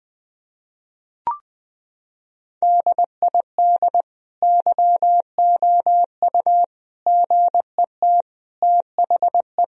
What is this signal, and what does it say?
Morse code, heard as a keyed tone, about 20 wpm, 700 Hz. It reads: DID YOU GET THE
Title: Sentences from Top 1000 Words - Review Part 3 - 20wpm